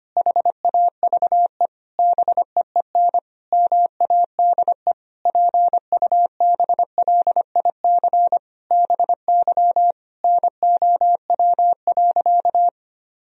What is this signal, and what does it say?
Morse code, heard as a keyed tone, about 25 wpm, 700 Hz. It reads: HAVE BEEN MADE PUBLIC BY NOW.
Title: Sentences from Top 1000 Words - Review Part 3 - 25wpm